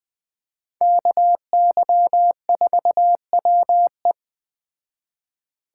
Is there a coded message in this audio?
Yes: Morse code from 0.8 to 4.1 s, heard as a keyed tone, about 19 wpm, 700 Hz. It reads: KY4WE